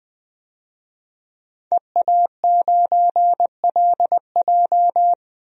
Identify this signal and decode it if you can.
Morse code, heard as a keyed tone, about 21 wpm, 700 Hz. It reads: EA9LJ
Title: DXCC-100 Call Signs - 5 Characters - 20wpm